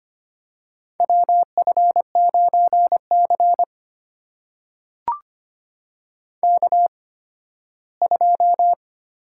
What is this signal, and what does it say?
Morse code, heard as a keyed tone, about 25 wpm, spaced as for 15 wpm, 700 Hz. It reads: WF9C K2
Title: USA Call Signs - 4 Characters - 8x Character Spacing - 25wpm